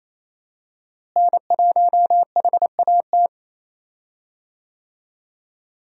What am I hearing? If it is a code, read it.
Morse code, heard as a keyed tone, about 28 wpm, 700 Hz. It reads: N1HAT
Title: USA Call Signs - 5 Characters - Repeated 2x - 28wpm